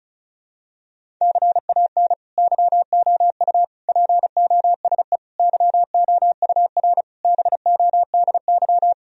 Morse code, heard as a keyed tone, about 34 wpm, 700 Hz. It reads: CAN YOU POSE YOUR BODY
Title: Sentences from Top 1000 Words - Review Part 2 - 35wpm